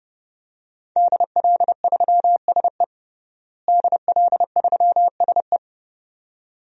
Morse code, heard as a keyed tone, about 30 wpm, 700 Hz. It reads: DL3SE DL3SE
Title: DXCC-100 Call Signs - 5 Characters - Repeated 2x - 30wpm